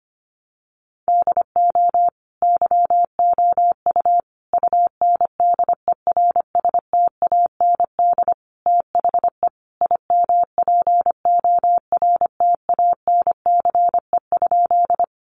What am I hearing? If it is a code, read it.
Morse code, heard as a keyed tone, about 25 wpm, 700 Hz. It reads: DO YOU UNDERSTAND THE IMPORTANCE?
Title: Sentences from Top 1200 Words 25wpm